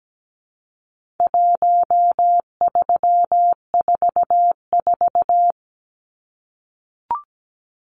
Morse code, heard as a keyed tone, about 17 wpm, 700 Hz. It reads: 1344
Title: Numbers - 4-Digit 17wpm